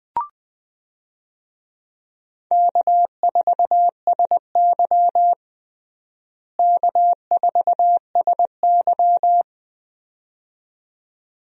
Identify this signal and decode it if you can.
Morse code, heard as a keyed tone, about 20 wpm, 700 Hz. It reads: K4SY K4SY
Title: USA Call Signs - 4 Characters - Repeated 2x - 20wpm